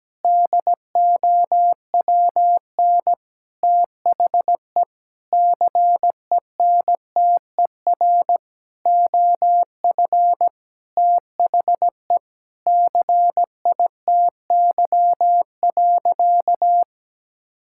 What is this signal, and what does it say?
Morse code, heard as a keyed tone, about 17 wpm, 700 Hz. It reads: DOWN THE CENTER OF THE CITY.
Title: Sentences from Top 1000 Words - Review Part 6 - 17wpm